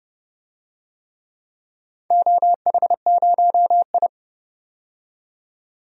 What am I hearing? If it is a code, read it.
Morse code, heard as a keyed tone, about 30 wpm, 700 Hz. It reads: OH0I
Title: DXCC-100 Call Signs - 4 Characters - 30wpm